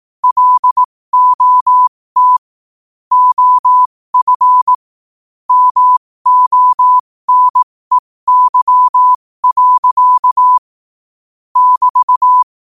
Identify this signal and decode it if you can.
Morse code, heard as a keyed tone, about 18 wpm, spaced as for 15 wpm, 1000 Hz. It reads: LOT OF MONEY. <BT>
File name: qotdfiles / 15wpm